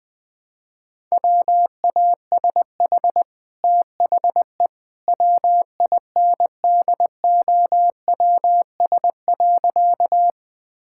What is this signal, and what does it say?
Morse code, heard as a keyed tone, about 20 wpm, 700 Hz. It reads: WASH THE WINDOWS.
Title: Easy Sentences with Hints - 20wpm